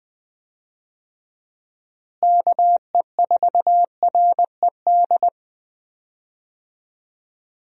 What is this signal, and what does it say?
Morse code, heard as a keyed tone, about 20 wpm, 700 Hz. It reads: KE4RED
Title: USA Call Signs - 6 Characters - Repeated 3x - 20wpm